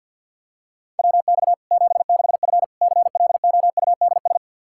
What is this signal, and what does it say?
Morse code, heard as a keyed tone, about 50 wpm, 700 Hz. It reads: WX 76F CLOUDS
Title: Practice QSO - Exchange 2 - Info - 50wpm